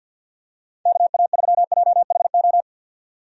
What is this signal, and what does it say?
Morse code, heard as a keyed tone, about 49 wpm, 700 Hz. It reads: KA3JHO
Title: USA Call Signs - 6 Characters - 50wpm